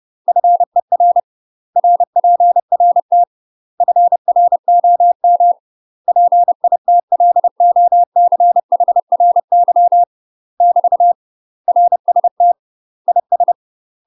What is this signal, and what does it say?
Morse code, heard as a keyed tone, about 30 wpm, 700 Hz. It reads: FER RPRT FROM PITLOCHRY <BT> RST IS
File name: UK MASTER QSOs TRAINING / 30WPM / UK MASTER QSOs 0049